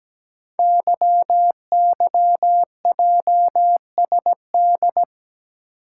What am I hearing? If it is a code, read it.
Morse code, heard as a keyed tone, about 17 wpm, 700 Hz. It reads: YYJSD
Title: Code Groups - 5 Letters - 17wpm